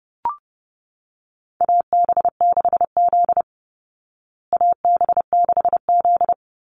Morse code, heard as a keyed tone, about 30 wpm, 700 Hz. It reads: AB6Z AB6Z
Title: Call Signs Repeated 2x - 30wpm